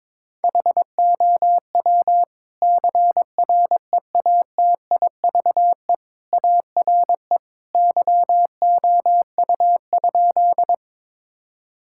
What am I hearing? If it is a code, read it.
Morse code, heard as a keyed tone, about 22 wpm, 700 Hz. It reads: HOW CREATIVE ARE YOU?
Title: Sentences from Top 1000 Words - Review Part 5 - 22wpm